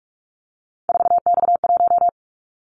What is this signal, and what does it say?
Morse code, heard as a keyed tone, about 44 wpm, 700 Hz. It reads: 4X1
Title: Callsign Prefixes for Top 50 DXCC - 45wpm